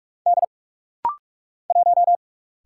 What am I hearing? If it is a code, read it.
Morse code, heard as a keyed tone, about 46 wpm, 700 Hz. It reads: D 1
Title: Single Letter-Number Alt - Rapid-Fire 45wpm